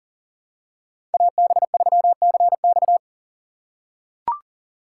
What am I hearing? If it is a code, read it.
Morse code, heard as a keyed tone, about 39 wpm, 700 Hz. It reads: AB3CX
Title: USA Call Signs - 5 Characters - 2x Character Spacing - 40wpm